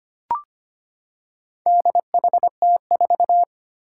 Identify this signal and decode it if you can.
Morse code, heard as a keyed tone, about 25 wpm, 700 Hz. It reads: DHT4
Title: Four Challenging Letter-Number Combinations 25wpm